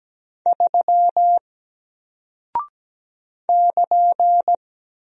Morse code, heard as a keyed tone, about 17 wpm, 700 Hz. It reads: 3 <KN>
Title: Single Character (Letter:Number:Punctuation:Prosign) 17wpm